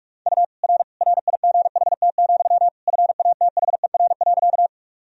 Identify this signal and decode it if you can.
Morse code, heard as a keyed tone, about 45 wpm, 700 Hz. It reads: U R RIGHT, FATHER.
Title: Sentences from Top 300 Words and Abbreviations 45wpm